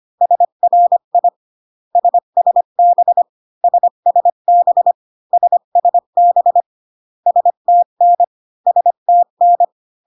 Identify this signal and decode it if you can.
Morse code, heard as a keyed tone, about 25 wpm, 700 Hz. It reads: SRI SSB SSB SSB STN STN